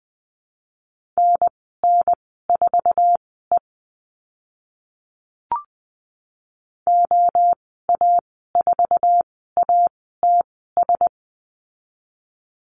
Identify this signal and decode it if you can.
Morse code, heard as a keyed tone, about 20 wpm, spaced as for 15 wpm, 700 Hz. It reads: NN4E OA4ATS
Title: 069 Lesson 12g - Letter d - Call Signs 20wpm